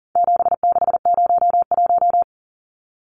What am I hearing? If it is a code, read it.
Morse code, heard as a keyed tone, about 40 wpm, 700 Hz. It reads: 7601
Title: Numbers - 4-Digit 40wpm